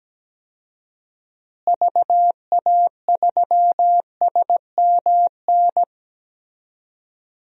Morse code, heard as a keyed tone, about 17 wpm, 700 Hz. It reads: VA3SMN
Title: DXCC-100 Call Signs - 6 Characters - Repeated 2x - 17wpm